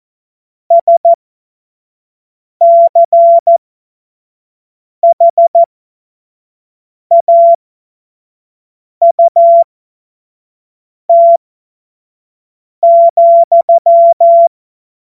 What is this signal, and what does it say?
Morse code, heard as a keyed tone, about 14 wpm, spaced as for 5 wpm, 680 Hz. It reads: SCHAUT,
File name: Morsekurs / Witze / Witz 01. Juni 2021, 25 F70